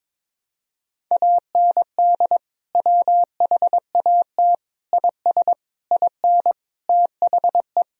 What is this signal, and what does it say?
Morse code, heard as a keyed tone, about 22 wpm, 700 Hz. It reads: AND WHAT IS IN THE
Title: Sentences from Top 1000 Words - Review Part 1 - 22wpm